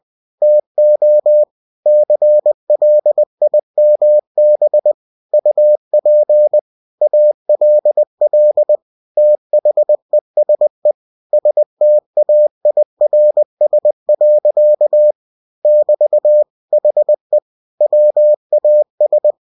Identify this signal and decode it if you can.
Morse code, heard as a keyed tone, about 20 wpm, 600 Hz. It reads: TO CLIMB UP ALL THESE STAIRS. <BT> HE WAS
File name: cw-books / winnie-the-pooh / winnie-20wpm-page-0003